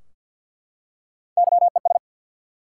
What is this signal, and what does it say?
Morse code, heard as a keyed tone, about 49 wpm, 700 Hz. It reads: YES
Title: Three Letter Words - 50wpm